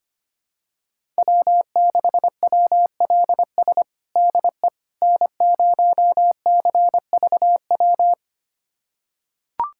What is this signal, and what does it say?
Morse code, heard as a keyed tone, about 25 wpm, 700 Hz. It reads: W6WLS DE N0CVW
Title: USA Callsign DE Callsign - 5 Characters - 25wpm